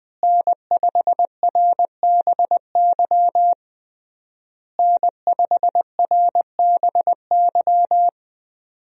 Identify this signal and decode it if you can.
Morse code, heard as a keyed tone, about 20 wpm, 700 Hz. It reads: N5RBY N5RBY
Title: USA Call Signs - 5 Characters - Repeated 3x - 20wpm